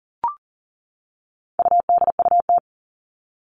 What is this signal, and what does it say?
Morse code, heard as a keyed tone, about 40 wpm, 700 Hz. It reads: UDUT
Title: All Cut Numbers - 4-Digit 40wpm